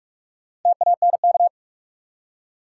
Morse code, heard as a keyed tone, about 46 wpm, 700 Hz. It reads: TANK
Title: Four Letter Words - 45wpm